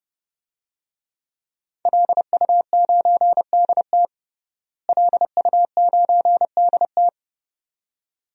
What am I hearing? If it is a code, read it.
Morse code, heard as a keyed tone, about 32 wpm, 700 Hz. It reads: LU9DT LU9DT
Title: DXCC-100 Call Signs - 5 Characters - Repeated 2x - 30wpm